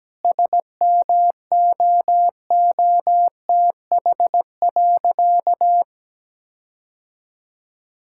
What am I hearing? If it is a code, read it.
Morse code, heard as a keyed tone, about 17 wpm, 700 Hz. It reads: SMOOTH.
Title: Sentences from Top 1600 Words 17wpm